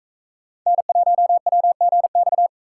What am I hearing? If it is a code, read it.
Morse code, heard as a keyed tone, about 41 wpm, 700 Hz. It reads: N1WGX